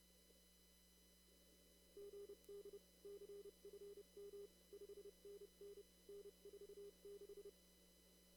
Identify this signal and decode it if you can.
Morse code, heard as a keyed tone, about 28 wpm, 400 Hz. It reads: GD CFM 5NN N46
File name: CW / kt5j wpx cw-op-k5pi